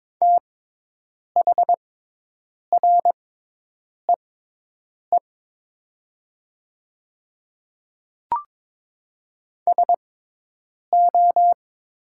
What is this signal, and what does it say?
Morse code, heard as a keyed tone, about 22 wpm, spaced as for 8 wpm, 700 Hz. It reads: THREE SO
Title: 6x Character Spacing - Top 200 Words - 22wpm